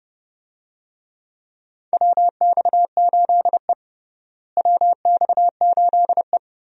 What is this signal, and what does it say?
Morse code, heard as a keyed tone, about 30 wpm, 700 Hz. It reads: WX8E WX8E
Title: USA Call Signs - 4 Characters - Repeated 2x - 30wpm